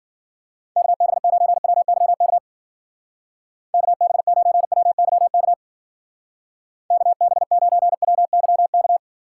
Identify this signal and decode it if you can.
Morse code, heard as a keyed tone, about 50 wpm, 700 Hz. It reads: KB9WYK KB9WYK KB9WYK
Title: Speed Racing - USA 2x3 Call Signs 40wpm